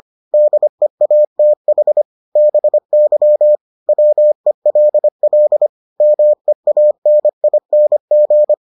Morse code, heard as a keyed tone, about 25 wpm, 600 Hz. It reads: DEATH BY WELL MEANING